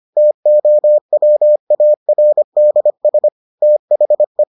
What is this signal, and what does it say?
Morse code, heard as a keyed tone, about 25 wpm, 600 Hz. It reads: TOWARDS THE